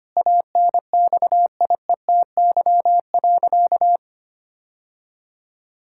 Morse code, heard as a keyed tone, about 25 wpm, 700 Hz. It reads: ANXIETY.
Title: Sentences from Top 1600 Words 25wpm